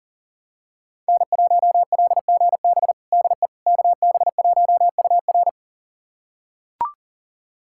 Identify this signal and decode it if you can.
Morse code, heard as a keyed tone, about 40 wpm, 700 Hz. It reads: N1LGB DE KB1UR
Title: USA Callsign DE Callsign - 5 Characters - 40wpm